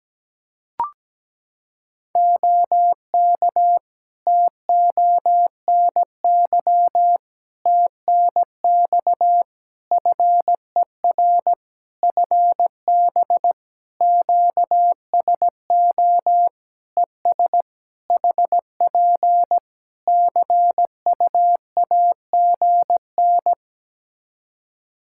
Morse code, heard as a keyed tone, about 17 wpm, 700 Hz. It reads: OK TONY TNX FER FB QSO ES HP CUAGN
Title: Practice QSO - Exchange 4 - Ending - 17wpm